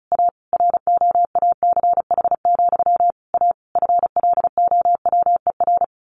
Morse code, heard as a keyed tone, about 34 wpm, 700 Hz. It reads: A ROACH, A FLOWER
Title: A Sound Of Thunder 35wpm